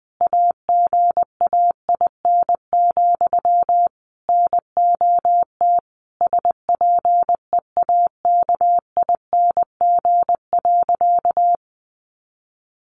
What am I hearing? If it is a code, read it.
Morse code, heard as a keyed tone, about 20 wpm, 700 Hz. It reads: AGAIN, NOT SPEAKING.
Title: A Sound Of Thunder 20wpm